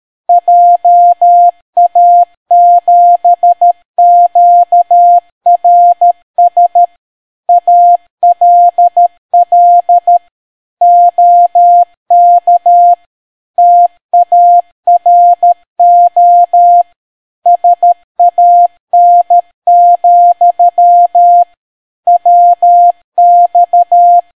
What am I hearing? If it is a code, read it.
Morse code, heard as a keyed tone, about 13 wpm, 700 Hz. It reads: JA7QRS ALL OK TARO SAN, WX